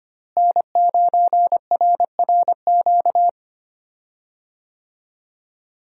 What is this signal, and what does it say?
Morse code, heard as a keyed tone, about 25 wpm, 700 Hz. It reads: N9RRQ
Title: USA Call Signs - 5 Characters - Repeated 2x - 25wpm